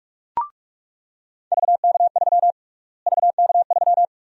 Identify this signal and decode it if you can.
Morse code, heard as a keyed tone, about 47 wpm, 700 Hz. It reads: VK3 VK3
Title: USA Call Sign Prefixes - Repeated 2x - 45wpm